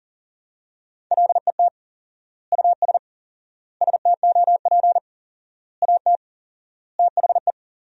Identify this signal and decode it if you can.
Morse code, heard as a keyed tone, about 40 wpm, 700 Hz. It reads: LET US STOP AT THE
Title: 4x Word-Spacing - Sentences from Top 400 Words - 40wpm